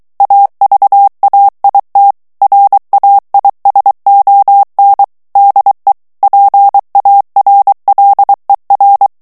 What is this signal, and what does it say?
Morse code, heard as a keyed tone, about 23 wpm, 800 Hz. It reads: AVAIT RAISON DE PARLER
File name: CW / FICHIERS-MP3 / FRANCAIS TEXTE CLAIR 2 VITESSE 22 MOTS MINUTES